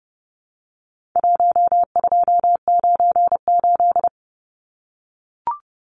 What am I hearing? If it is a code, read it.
Morse code, heard as a keyed tone, about 30 wpm, 700 Hz. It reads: 1298
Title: Numbers - 4-Digit 30wpm